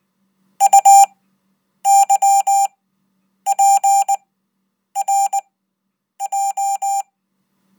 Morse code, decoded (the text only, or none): UYPRJ